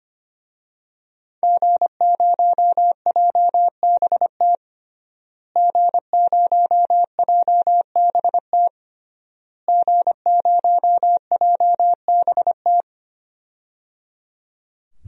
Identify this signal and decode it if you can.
Morse code, heard as a keyed tone, about 25 wpm, 700 Hz. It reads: G0JBT G0JBT G0JBT
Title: DXCC-100 Call Signs - 5 Characters - Repeated 3x - 25wpm